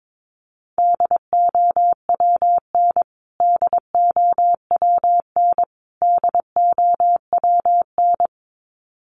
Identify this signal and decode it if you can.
Morse code, heard as a keyed tone, about 22 wpm, 700 Hz. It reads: DOWN DOWN DOWN
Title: Top 300 Words Repeated 22wpm